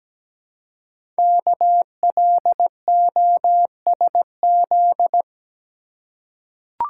Morse code, heard as a keyed tone, about 17 wpm, 700 Hz. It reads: KLOSZ
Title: Code Groups - 5 Letters - 17wpm